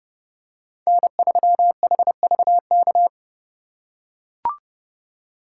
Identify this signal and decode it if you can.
Morse code, heard as a keyed tone, about 31 wpm, 700 Hz. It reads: N3HVK